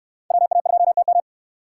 Morse code, heard as a keyed tone, about 69 wpm, 700 Hz. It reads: WA2TM